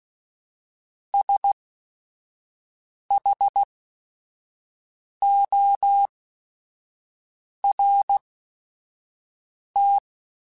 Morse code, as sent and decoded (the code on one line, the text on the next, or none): ... .... --- .-. -
SHORT